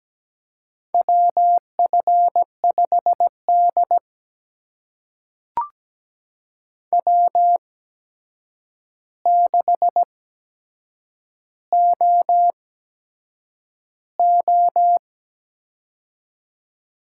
Morse code, heard as a keyed tone, about 17 wpm, spaced as for 7 wpm, 700 Hz. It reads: WF5D W6OO
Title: USA Call Signs - 4 Characters - 8x Character Spacing - 17wpm